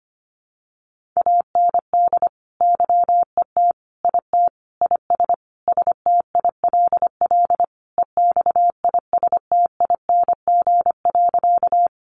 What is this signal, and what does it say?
Morse code, heard as a keyed tone, about 25 wpm, 700 Hz. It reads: AND YET IT IS STILL EXISTING.
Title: Sentences from Top 1500 Words 25wpm